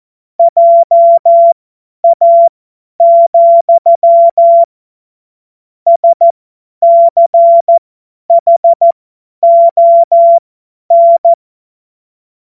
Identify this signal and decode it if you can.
Morse code, heard as a keyed tone, about 14 wpm, spaced as for 10 wpm, 680 Hz. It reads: JA, SCHON
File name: Morsekurs / Witze / Witz 21. März 2023 50 F70